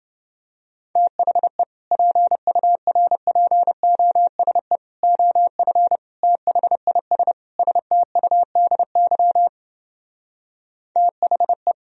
Morse code, heard as a keyed tone, about 29 wpm, 700 Hz. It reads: THE PURPOSE OF THIS STUDY THE